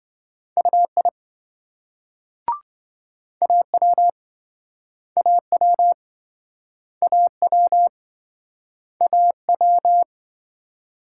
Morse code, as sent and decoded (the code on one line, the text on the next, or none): ..- .. / .- .-- / .- .-- / .- .-- / .- .--
UI AW AW AW AW